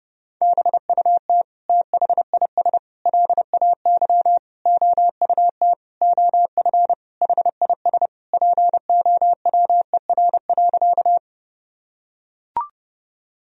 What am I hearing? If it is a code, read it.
Morse code, heard as a keyed tone, about 30 wpm, 700 Hz. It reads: BUT THIS LAY OUT OF HIS POWER.
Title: Sentences from Top 1000 Words - Review Part 2 - 30wpm